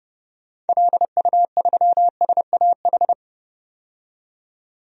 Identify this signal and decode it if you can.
Morse code, heard as a keyed tone, about 29 wpm, 700 Hz. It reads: LU3SAH
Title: DXCC-100 Call Signs - 6 Characters - 30wpm